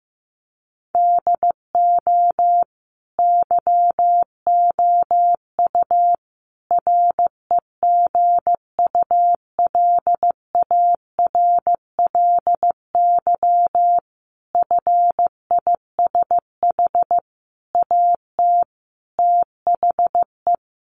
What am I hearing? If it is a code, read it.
Morse code, heard as a keyed tone, about 15 wpm, 700 Hz. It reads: DO YOU REGULARLY FISH AT THE
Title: Sentences from Top 1400 Words 15wpm